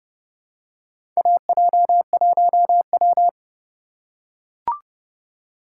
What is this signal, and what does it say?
Morse code, heard as a keyed tone, about 30 wpm, 700 Hz. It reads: AJ1W
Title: USA Call Signs - 4 Characters - 3x Character Spacing - 30wpm